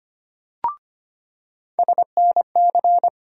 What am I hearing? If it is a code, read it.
Morse code, heard as a keyed tone, about 25 wpm, 700 Hz. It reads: SNC
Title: USA Call Sign Postfixes - 3 Letters 25wpm